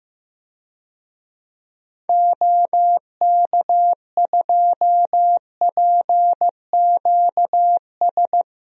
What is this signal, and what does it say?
Morse code, heard as a keyed tone, about 15 wpm, 700 Hz. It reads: OK2PQS
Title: DXCC-100 Call Signs - 6 Characters - 15wpm